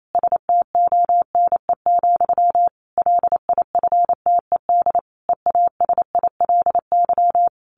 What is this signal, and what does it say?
Morse code, heard as a keyed tone, about 28 wpm, 700 Hz. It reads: STONE, LIFTED EASILY
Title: A Sound Of Thunder 28wpm